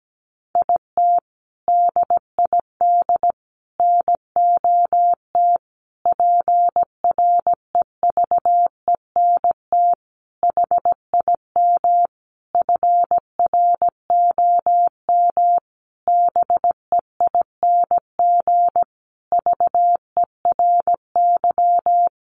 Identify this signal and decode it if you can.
Morse code, heard as a keyed tone, about 17 wpm, 700 Hz. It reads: IT DID NOT PREVENT HIM FROM BEING VERY
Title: Sentences from Top 1500 Words 17wpm